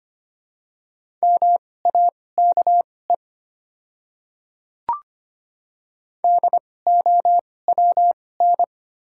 Morse code, heard as a keyed tone, about 25 wpm, spaced as for 18 wpm, 700 Hz. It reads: MAKE DOWN
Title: 2x Character Spacing - Top 100 Words - 25wpm